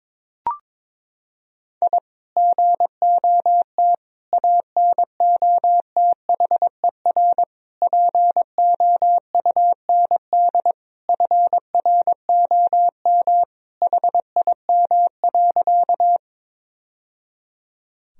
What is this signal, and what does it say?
Morse code, heard as a keyed tone, about 22 wpm, 700 Hz. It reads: I GOT ANOTHER POUND FROM HIM.
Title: Sentences from Top 1000 Words - Review Part 2 - 22wpm